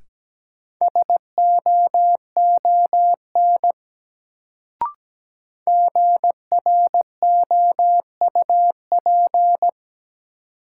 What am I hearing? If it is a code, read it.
Morse code, heard as a keyed tone, about 17 wpm, 700 Hz. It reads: SOON GROUP
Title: Sets of 3 Words - New Word Drill - 17wpm